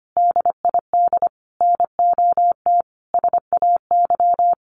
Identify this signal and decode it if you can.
Morse code, heard as a keyed tone, about 25 wpm, 700 Hz. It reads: DID NOT SAY
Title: Sets of 5 Words Encore2 25wpm